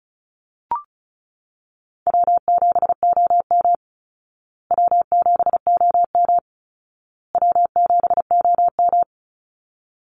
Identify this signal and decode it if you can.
Morse code, heard as a keyed tone, about 34 wpm, 700 Hz. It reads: W7OM W7OM W7OM
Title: Easy Call Signs Repeated 3x - 35wpm